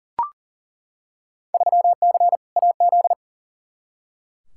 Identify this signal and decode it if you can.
Morse code, heard as a keyed tone, about 40 wpm, 700 Hz. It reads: 3C AZ
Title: Field Day Exchanges Encore - 40wpm